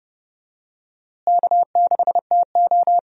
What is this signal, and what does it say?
Morse code, heard as a keyed tone, about 32 wpm, 700 Hz. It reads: K6TO
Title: USA Call Signs - 4 Characters - Repeated 2x - 30wpm